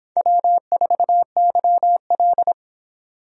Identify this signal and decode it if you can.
Morse code, heard as a keyed tone, about 23 wpm, 700 Hz. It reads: W4YL